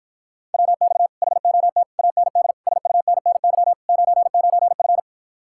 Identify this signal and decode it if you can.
Morse code, heard as a keyed tone, about 54 wpm, 700 Hz. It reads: WX HOT AND SUNNY 99F